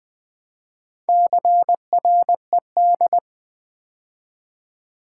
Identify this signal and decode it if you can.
Morse code, heard as a keyed tone, about 20 wpm, 700 Hz. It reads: CRED
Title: Build A Word - 2 Letters At A Time - 8-Letter Words - 20wpm